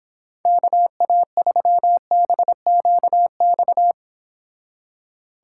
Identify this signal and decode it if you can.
Morse code, heard as a keyed tone, about 26 wpm, 700 Hz. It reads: KA3BQX